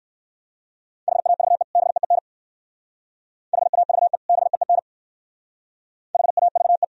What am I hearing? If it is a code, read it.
Morse code, heard as a keyed tone, about 69 wpm, 690 Hz. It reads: HAVE BEEN HAVE BEEN HAVE